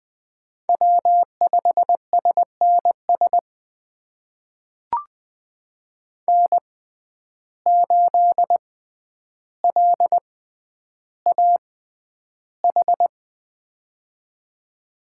Morse code, heard as a keyed tone, about 20 wpm, spaced as for 10 wpm, 700 Hz. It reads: W5SNS N8LAH
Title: USA Call Signs - 5 Characters - 6x Character Spacing - 20wpm